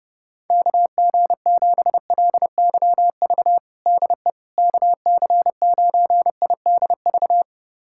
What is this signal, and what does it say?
Morse code, heard as a keyed tone, about 32 wpm, 700 Hz. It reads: KG7LYV DE KC9IDV
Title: USA Callsign DE Callsign - 6 Characters - 30wpm